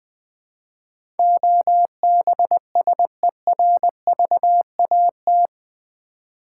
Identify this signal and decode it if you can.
Morse code, heard as a keyed tone, about 20 wpm, 700 Hz. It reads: OBSERVAT
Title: Build A Word - 4 Letters At A Time - 11-Letter Words - 20wpm